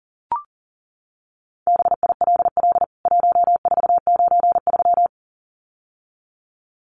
Morse code, heard as a keyed tone, about 40 wpm, 700 Hz. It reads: BILL 1493
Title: CWT Exchange 40wpm